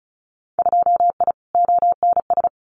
Morse code, heard as a keyed tone, about 34 wpm, 700 Hz. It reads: 2I ONS
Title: Winter Field Day Exchanges Encore 35wpm